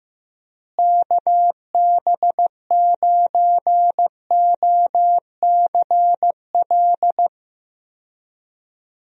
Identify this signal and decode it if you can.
Morse code, heard as a keyed tone, about 15 wpm, 700 Hz. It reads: KB9OCL
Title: USA Call Signs - 6 Characters - Repeated 3x - 15wpm